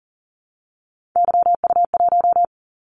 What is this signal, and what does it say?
Morse code, heard as a keyed tone, about 41 wpm, 700 Hz. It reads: YU1
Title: Callsign Prefixes for Top 25 DXCC - 40wpm